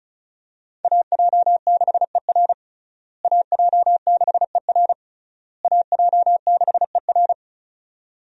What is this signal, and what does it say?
Morse code, heard as a keyed tone, about 36 wpm, 700 Hz. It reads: AJ6ER AJ6ER AJ6ER
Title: USA Call Signs - 5 Characters - Repeated 3x - 35wpm